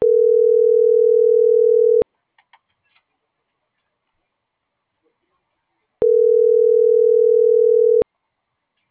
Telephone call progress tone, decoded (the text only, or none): ringback tone